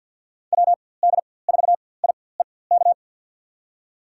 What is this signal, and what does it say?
Morse code, heard as a keyed tone, about 48 wpm, spaced as for 23 wpm, 700 Hz. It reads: WD4IEK
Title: USA Call Signs - 6 Characters - 4x Character Spacing - 50wpm